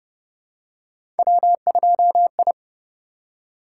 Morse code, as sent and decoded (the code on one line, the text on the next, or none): .-- ..--- ..
W2I